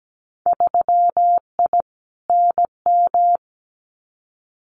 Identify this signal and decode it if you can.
Morse code, heard as a keyed tone, about 17 wpm, 700 Hz. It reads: 3I NM
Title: Winter Field Day Exchanges Encore 17wpm